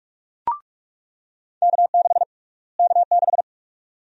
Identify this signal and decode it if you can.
Morse code, heard as a keyed tone, about 42 wpm, 700 Hz. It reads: K6 K6
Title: USA Call Sign Prefixes - Repeated 2x - 45wpm